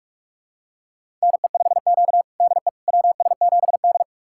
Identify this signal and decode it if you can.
Morse code, heard as a keyed tone, about 43 wpm, 700 Hz. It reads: NE5Q DE WS7D
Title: USA Callsign DE Callsign - 4 Characters - 45wpm